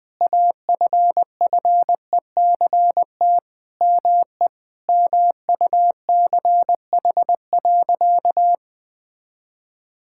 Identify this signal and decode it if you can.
Morse code, heard as a keyed tone, about 21 wpm, 700 Hz. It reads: AFFECT ME MUCH.
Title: Sentences from Top 1000 Words - Review Part 6 - 20wpm